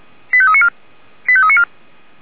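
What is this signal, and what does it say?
Five-tone selective call (ZVEI), heard as quick runs of five tones, tones of about 70 ms. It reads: 85394 85394
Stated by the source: ZVEI five-tone call received by SDR.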